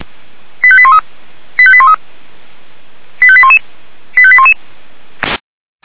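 Five-tone selective call (ZVEI), heard as quick runs of five tones, tones of about 70 ms. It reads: 85813 85813 85811 85811
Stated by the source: ZVEI five-tone selective call received by SDR.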